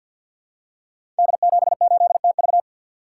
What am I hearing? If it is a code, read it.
Morse code, heard as a keyed tone, about 51 wpm, 700 Hz. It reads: D78TV
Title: Code Groups - 5 Letter:Numbers - 50wpm